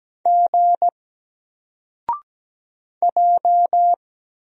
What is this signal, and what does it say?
Morse code, heard as a keyed tone, about 17 wpm, 700 Hz. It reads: G J
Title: Single Letter-Number Alt 17wpm